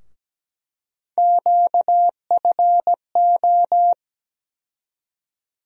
Morse code, heard as a keyed tone, about 17 wpm, 700 Hz. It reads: QFO
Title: USA Call Sign Postfixes - 3 Letters 17wpm